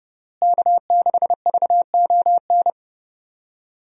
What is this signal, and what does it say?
Morse code, heard as a keyed tone, about 32 wpm, 700 Hz. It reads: K6VON
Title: USA Call Signs - 5 Characters - Repeated 3x - 30wpm